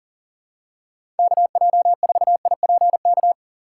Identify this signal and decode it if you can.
Morse code, heard as a keyed tone, about 40 wpm, 700 Hz. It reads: KJ4IPK
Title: USA Call Signs - 6 Characters - 4x Character Spacing - 40wpm